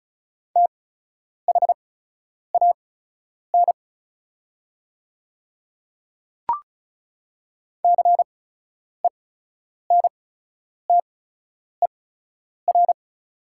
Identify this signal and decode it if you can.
Morse code, heard as a keyed tone, about 36 wpm, spaced as for 10 wpm, 700 Hz. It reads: THAN CENTER
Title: 8x Character Spacing - Top 500 Words - 35wpm